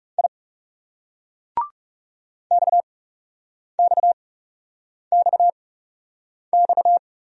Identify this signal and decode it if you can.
Morse code, heard as a keyed tone, about 39 wpm, spaced as for 8 wpm, 700 Hz. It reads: I XXXX